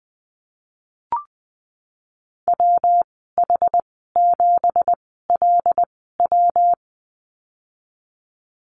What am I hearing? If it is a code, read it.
Morse code, heard as a keyed tone, about 20 wpm, spaced as for 15 wpm, 700 Hz. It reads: WH7LW
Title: 167 Lesson 27g - Number 7 - Call Signs 20wpm